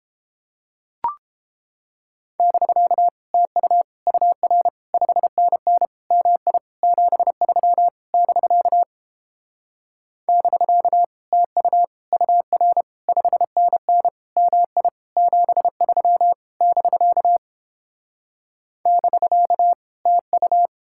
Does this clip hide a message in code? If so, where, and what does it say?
Yes: Morse code from 2.4 to 20.7 s, heard as a keyed tone, about 31 wpm, 700 Hz. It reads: <BK> TU UR 5NN MI 73 <BK> <BK> TU UR 5NN MI 73 <BK> <BK> TU